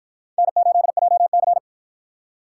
Morse code, heard as a keyed tone, about 53 wpm, 700 Hz. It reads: N8JC